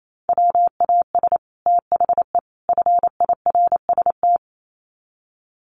Morse code, heard as a keyed tone, about 28 wpm, 700 Hz. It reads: WAS THE FIRST
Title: Sets of 3 Words 28wpm